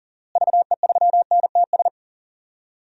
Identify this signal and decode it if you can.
Morse code, heard as a keyed tone, about 37 wpm, 700 Hz. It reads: VE3NTS
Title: DXCC-100 Call Signs - 6 Characters - Repeated 3x - 40wpm